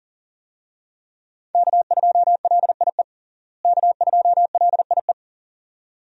Morse code, heard as a keyed tone, about 40 wpm, 700 Hz. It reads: K2LIE K2LIE
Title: USA Call Signs - 5 Characters - Repeated 2x - 40wpm